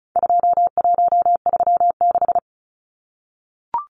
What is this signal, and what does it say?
Morse code, heard as a keyed tone, about 34 wpm, 700 Hz. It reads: 2136
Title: Numbers - 4-Digit 35wpm